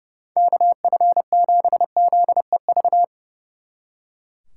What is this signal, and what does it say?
Morse code, heard as a keyed tone, about 38 wpm, 700 Hz. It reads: KF7ZEV